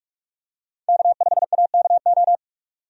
Morse code, heard as a keyed tone, about 44 wpm, 700 Hz. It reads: K5AKO
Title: USA Call Signs - 5 Characters - 45wpm